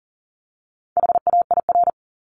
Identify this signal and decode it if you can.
Morse code, heard as a keyed tone, about 40 wpm, 700 Hz. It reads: HAIR
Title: Top 1751-2000 Words 40wpm